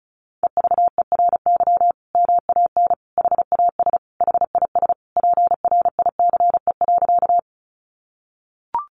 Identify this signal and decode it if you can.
Morse code, heard as a keyed tone, about 34 wpm, 700 Hz. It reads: EVERY MAN HAS HIS PRICE.
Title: Sentences from Top 600 Words 35wpm